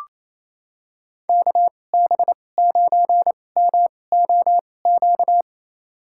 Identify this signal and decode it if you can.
Morse code, heard as a keyed tone, about 28 wpm, spaced as for 20 wpm, 700 Hz. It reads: KB9MOQ
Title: USA Call Signs - 6 Characters - 2x Character Spacing - 28wpm